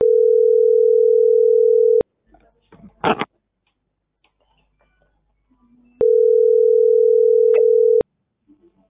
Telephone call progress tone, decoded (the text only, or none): ringback tone